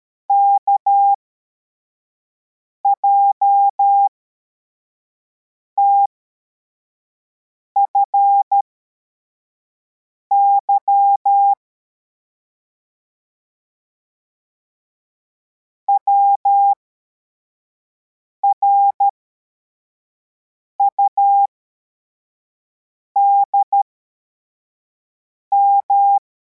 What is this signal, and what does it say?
Morse code, heard as a keyed tone, about 13 wpm, spaced as for 4 wpm, 800 Hz. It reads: KJTFY WRUDM
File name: formation / cw / Text85